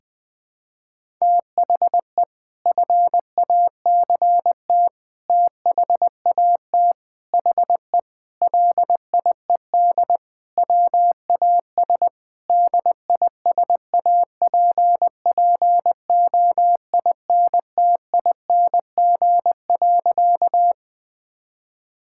Morse code, heard as a keyed tone, about 20 wpm, 700 Hz. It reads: THE FACT THAT HE LIED WAS DISAPPOINTING.
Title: Sentences with 4-Word Phrases - 20wpm